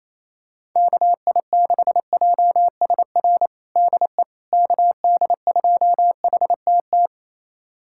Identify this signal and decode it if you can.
Morse code, heard as a keyed tone, about 28 wpm, 700 Hz. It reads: KI6JSR DE KD2HTT
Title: USA Callsign DE Callsign - 6 Characters - 28wpm